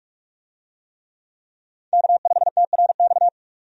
Morse code, heard as a keyed tone, about 46 wpm, 700 Hz. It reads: K5TRX
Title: USA Call Signs - 5 Characters - 45wpm